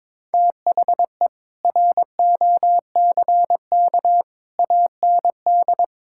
Morse code, heard as a keyed tone, about 22 wpm, 700 Hz. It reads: THE ROCK AND
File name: practice / 22wpm / Sets of 7 Words - 22wpm